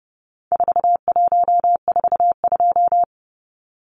Morse code, heard as a keyed tone, about 30 wpm, 700 Hz. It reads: 4142
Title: Numbers - 4-Digit 30wpm